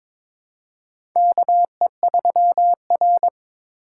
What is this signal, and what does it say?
Morse code, heard as a keyed tone, about 23 wpm, 700 Hz. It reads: KE3R
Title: USA Call Signs - 4 Characters - 3x Character Spacing - 22wpm